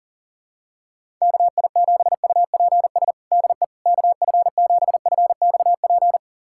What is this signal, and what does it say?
Morse code, heard as a keyed tone, about 40 wpm, 700 Hz. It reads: KI7UPS DE KF7FXP
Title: USA Callsign DE Callsign - 6 Characters - 40wpm